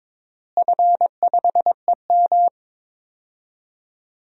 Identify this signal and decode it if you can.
Morse code, heard as a keyed tone, about 22 wpm, 700 Hz. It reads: F5EM
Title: DXCC-100 Call Signs - 4 Characters - Repeated 3x - 22wpm